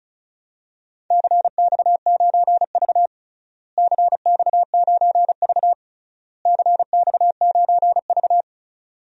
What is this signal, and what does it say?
Morse code, heard as a keyed tone, about 34 wpm, 700 Hz. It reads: CX9V CX9V CX9V
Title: DXCC-100 Call Signs - 4 Characters - Repeated 3x - 35wpm